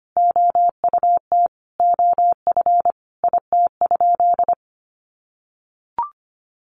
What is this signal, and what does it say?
Morse code, heard as a keyed tone, about 25 wpm, 700 Hz. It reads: OUT OF IT?
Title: Sentences from Top 300 Words 25wpm